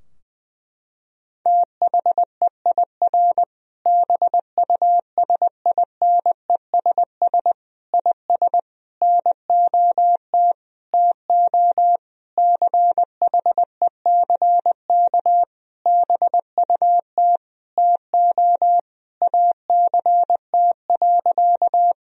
Morse code, heard as a keyed tone, about 20 wpm, 700 Hz. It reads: THEIR BUSINESS IS NOT TO CHECK BUT TO ACT.
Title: Sentences from Top 1000 Words - Review Part 3 - 20wpm